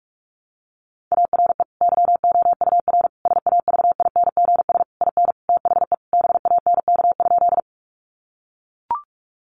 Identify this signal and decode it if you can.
Morse code, heard as a keyed tone, about 44 wpm, 700 Hz. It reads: ARE YOUR SAVINGS IN THE BANK?
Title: Sentences from Top 1200 Words 45wpm